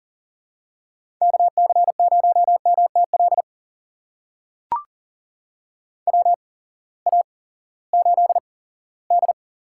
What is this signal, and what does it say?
Morse code, heard as a keyed tone, about 40 wpm, spaced as for 20 wpm, 700 Hz. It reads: KC0MTL WA8D
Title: USA Call Signs - 6 Characters - 8x Character Spacing - 40wpm